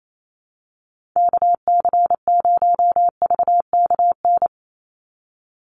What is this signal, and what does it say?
Morse code, heard as a keyed tone, about 28 wpm, 700 Hz. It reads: KC0VKN
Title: Call Signs Repeated 2x - 28wpm